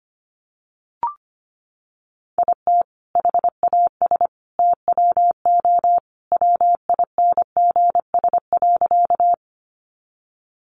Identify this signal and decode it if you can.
Morse code, heard as a keyed tone, about 25 wpm, 700 Hz. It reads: IT HAS TWO WINGS.
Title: Additional Easy Phrases 25wpm